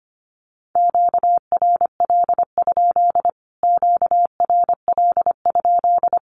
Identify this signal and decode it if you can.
Morse code, heard as a keyed tone, about 25 wpm, 700 Hz. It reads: QRL? QRL?
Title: Q-Codes Repeated 3x 25wpm